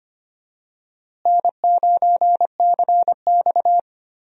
Morse code, heard as a keyed tone, about 25 wpm, 700 Hz. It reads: N9CX
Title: USA Call Signs - 4 Characters - Repeated 2x - 25wpm